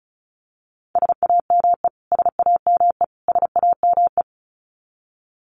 Morse code, heard as a keyed tone, about 34 wpm, 700 Hz. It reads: SAME SAME SAME
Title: Top 500 Words Repeated 35wpm